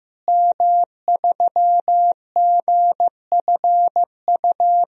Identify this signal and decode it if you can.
Morse code, heard as a keyed tone, about 15 wpm, 700 Hz. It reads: M3GFU
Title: DXCC-100 Call Signs - 5 Characters - Repeated 2x - 15wpm